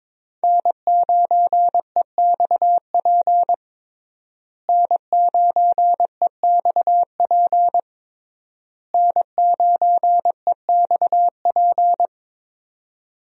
Morse code, heard as a keyed tone, about 22 wpm, 700 Hz. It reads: N9EXP N9EXP N9EXP
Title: USA Call Signs - 5 Characters - Repeated 3x - 22wpm